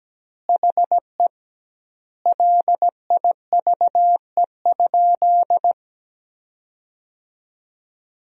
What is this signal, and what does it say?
Morse code, heard as a keyed tone, about 17 wpm, 700 Hz. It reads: HE LIVE?
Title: 2x Word-Spacing - Sentences from Top 200 Words - 17wpm